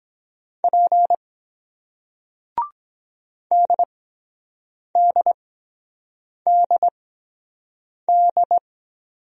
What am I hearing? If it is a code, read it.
Morse code, heard as a keyed tone, about 23 wpm, 700 Hz. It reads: P D D D D